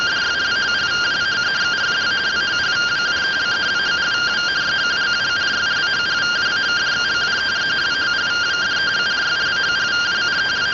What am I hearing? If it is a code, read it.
Two-tone teleprinter signal, about 170 Hz shift.
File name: findings / 81 axial iodine 262.gen-08 rtty rx